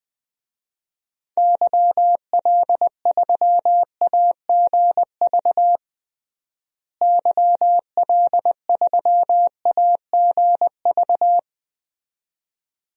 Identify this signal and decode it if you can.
Morse code, heard as a keyed tone, about 20 wpm, 700 Hz. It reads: YL3AGV YL3AGV
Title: DXCC-100 Call Signs - 6 Characters - Repeated 2x - 20wpm